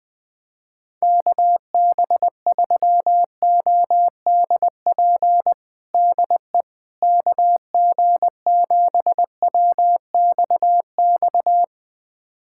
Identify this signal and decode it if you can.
Morse code, heard as a keyed tone, about 20 wpm, 700 Hz. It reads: KB3ODP DE KG7WXX
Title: USA Callsign DE Callsign 20wpm